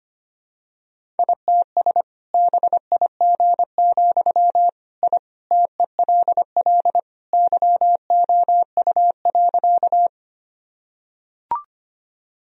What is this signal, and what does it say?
Morse code, heard as a keyed tone, about 25 wpm, 700 Hz. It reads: ITS BIG, I TELL YOU.
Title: Sentences from Top 1000 Words - Review Part 1 - 25wpm